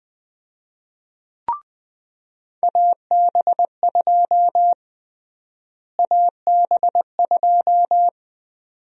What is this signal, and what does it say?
Morse code, heard as a keyed tone, about 20 wpm, 700 Hz. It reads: AB2 AB2
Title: USA Call Sign Prefixes - Repeated 2x - 20wpm